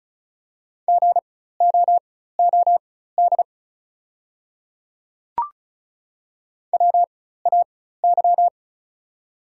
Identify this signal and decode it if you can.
Morse code, heard as a keyed tone, about 34 wpm, spaced as for 16 wpm, 700 Hz. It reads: GOOD WAY
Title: 4x Character Spacing - Top 200 Words - 35wpm